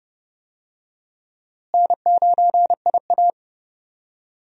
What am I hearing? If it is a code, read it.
Morse code, heard as a keyed tone, about 32 wpm, 700 Hz. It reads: N9IA
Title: USA Call Signs - 4 Characters - Repeated 2x - 30wpm